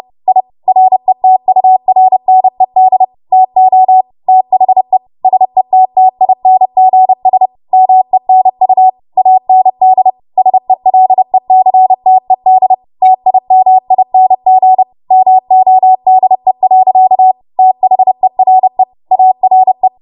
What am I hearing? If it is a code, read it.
Morse code, heard as a keyed tone, about 30 wpm, 750 Hz. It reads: I RETURNED TO THE SETTINGS MENU AND SELECTED TIMING MODE. THERE ARE
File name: Morse / 30WPM